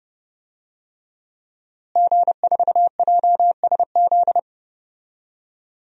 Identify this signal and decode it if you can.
Morse code, heard as a keyed tone, about 29 wpm, 700 Hz. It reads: G4JSZ
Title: DXCC-100 Call Signs - 5 Characters - Repeated 3x - 30wpm